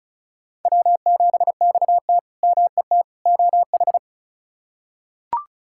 Morse code, heard as a keyed tone, about 36 wpm, 700 Hz. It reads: W7XT MET OH
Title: K1USN SST QSO Practice - 35wpm